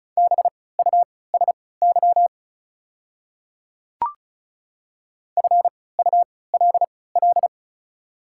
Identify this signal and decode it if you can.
Morse code, heard as a keyed tone, about 34 wpm, spaced as for 20 wpm, 700 Hz. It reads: BUSY FULL
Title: 3x Character Spacing - Top 500 Words - 35wpm